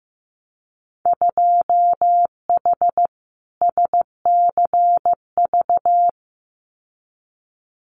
Morse code, heard as a keyed tone, about 15 wpm, 700 Hz. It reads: 2H SCV
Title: Winter Field Day Exchanges Encore 15wpm